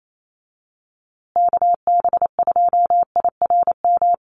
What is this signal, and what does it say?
Morse code, heard as a keyed tone, about 28 wpm, 700 Hz. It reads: KB2IRM
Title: USA Call Signs Repeated 2x 28wpm